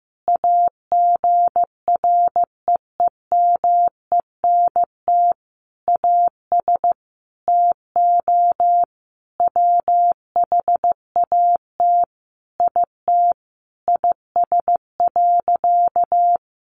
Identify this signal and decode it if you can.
Morse code, heard as a keyed tone, about 15 wpm, 700 Hz. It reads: AGREEMENT AS TO WHAT IT IS.
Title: Sentences from Top 1500 Words 15wpm